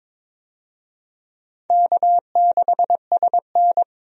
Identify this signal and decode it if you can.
Morse code, heard as a keyed tone, about 22 wpm, 700 Hz. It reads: K6SN
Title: USA Call Signs - 4 Characters - 22wpm